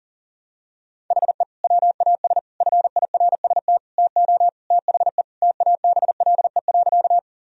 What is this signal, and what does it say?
Morse code, heard as a keyed tone, about 40 wpm, 700 Hz. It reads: HE WAS FIRST TO THE TABLE.
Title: Sentences from Top 1000 Words - Review Part 2 - 40wpm